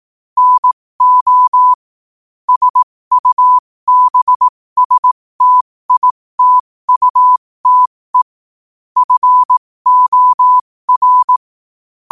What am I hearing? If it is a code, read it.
Morse code, heard as a keyed tone, about 18 wpm, spaced as for 15 wpm, 1000 Hz. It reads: NO SUBSTITUTE FOR
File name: qotdfiles / 15wpm